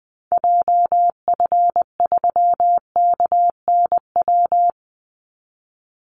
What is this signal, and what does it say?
Morse code, heard as a keyed tone, about 20 wpm, 700 Hz. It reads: JF3KNW
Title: Call Signs Repeated 3x - 20wpm